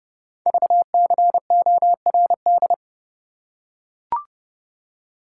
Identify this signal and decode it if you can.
Morse code, heard as a keyed tone, about 30 wpm, 700 Hz. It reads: VCORD
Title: Code Groups - 5 Letters - 30wpm